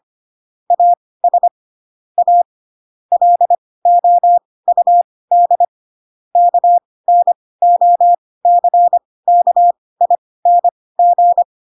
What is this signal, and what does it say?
Morse code, heard as a keyed tone, about 25 wpm, spaced as for 18 wpm, 700 Hz. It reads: AS A LOUD KNOCKING
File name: Pinocchio / Pinocchio ch6 12 25 ik1hge